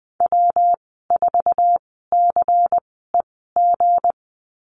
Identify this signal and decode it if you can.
Morse code, heard as a keyed tone, about 20 wpm, spaced as for 15 wpm, 700 Hz. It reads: W4CEG
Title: 161 Lesson 26g - Letter g - Call Signs 20wpm